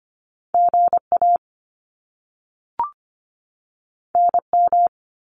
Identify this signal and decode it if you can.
Morse code, heard as a keyed tone, about 25 wpm, 700 Hz. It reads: GA NM
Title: US States 25wpm